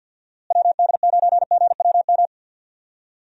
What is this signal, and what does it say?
Morse code, heard as a keyed tone, about 50 wpm, 700 Hz. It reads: WD9GWM
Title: USA Call Signs - 6 Characters - Repeated 2x - 50wpm